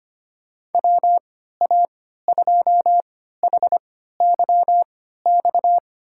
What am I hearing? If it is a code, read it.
Morse code, heard as a keyed tone, about 25 wpm, spaced as for 14 wpm, 700 Hz. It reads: WA2HYX
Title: USA Call Signs - 6 Characters - 3x Character Spacing - 25wpm